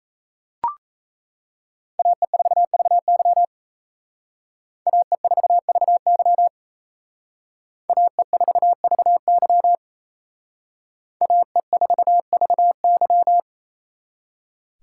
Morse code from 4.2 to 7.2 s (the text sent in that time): AE4VY